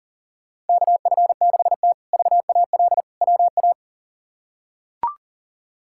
Morse code, heard as a keyed tone, about 37 wpm, 700 Hz. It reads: KF6T VAL WA
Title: K1USN SST QSO Practice - 40wpm